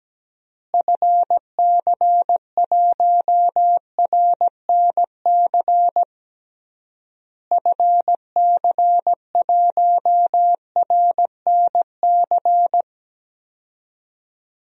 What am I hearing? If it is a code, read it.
Morse code, heard as a keyed tone, about 17 wpm, 700 Hz. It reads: FC1RNC FC1RNC
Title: DXCC-100 Call Signs - 6 Characters - Repeated 3x - 17wpm